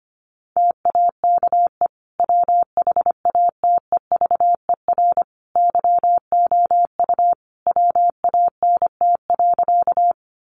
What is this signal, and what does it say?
Morse code, heard as a keyed tone, about 25 wpm, 700 Hz. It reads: TAKE WHATEVER YOU WANT.
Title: Sentences from Top 1100 Words 25wpm